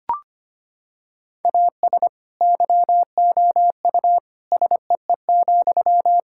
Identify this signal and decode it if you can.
Morse code, heard as a keyed tone, about 25 wpm, 700 Hz. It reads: AS YOU SEE,
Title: Sentences from Top 1000 Words - Review Part 6 - 25wpm